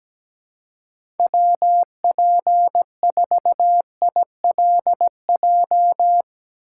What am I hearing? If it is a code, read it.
Morse code, heard as a keyed tone, about 17 wpm, 700 Hz. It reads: WP4ILJ
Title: USA Call Signs - 6 Characters - 3x Character Spacing - 17wpm